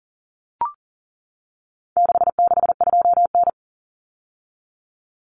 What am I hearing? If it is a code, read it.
Morse code, heard as a keyed tone, about 40 wpm, 700 Hz. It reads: 662N
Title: Common Cut Numbers - 4-Digit 40wpm